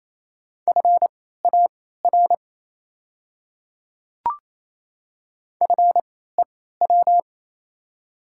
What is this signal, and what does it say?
Morse code, heard as a keyed tone, about 27 wpm, spaced as for 16 wpm, 700 Hz. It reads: FAR FEW
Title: 3x Character Spacing - Top 300 Words - 28wpm